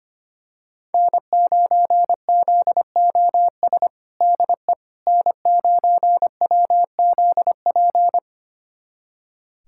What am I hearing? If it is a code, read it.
Morse code, heard as a keyed tone, about 25 wpm, 700 Hz. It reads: N9ZOS DE N9WZP
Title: USA Callsign DE Callsign - 5 Characters - 25wpm